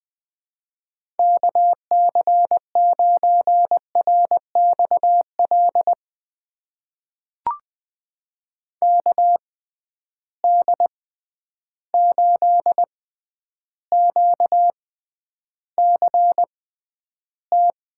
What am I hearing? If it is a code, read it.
Morse code, heard as a keyed tone, about 21 wpm, spaced as for 10 wpm, 700 Hz. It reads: KC9RXL KD8QCT
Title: USA Call Signs - 6 Characters - 6x Character Spacing - 20wpm